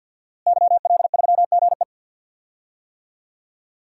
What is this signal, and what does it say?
Morse code, heard as a keyed tone, about 51 wpm, 700 Hz. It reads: YL3GE
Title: DXCC-100 Call Signs - 5 Characters - Repeated 3x - 50wpm